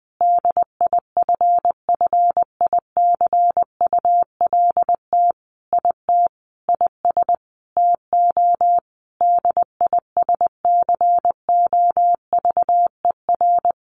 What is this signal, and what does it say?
Morse code, heard as a keyed tone, about 20 wpm, 700 Hz. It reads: DIFFICULT IT IS TO DISCOVER
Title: Sentences from Top 1100 Words 20wpm